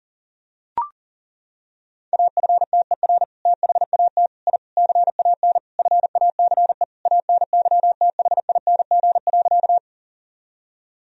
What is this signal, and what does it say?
Morse code, heard as a keyed tone, about 40 wpm, 700 Hz. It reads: AFTER THAT I CAN FACE ANYTHING.
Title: Sentences from Top 1000 Words - Review Part 2 - 40wpm